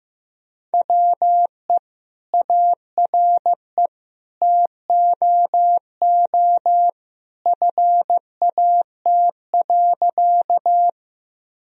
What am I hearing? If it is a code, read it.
Morse code, heard as a keyed tone, about 14 wpm, 700 Hz. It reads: WE ARE TOO FAT.
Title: Sentences from Top 1000 Words - Review Part 4 - 15wpm